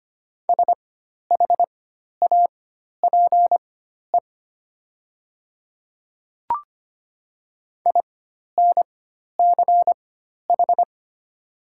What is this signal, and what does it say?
Morse code, heard as a keyed tone, about 25 wpm, spaced as for 12 wpm, 700 Hz. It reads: SHAPE INCH
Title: 4x Character Spacing - Top 500 Words - 25wpm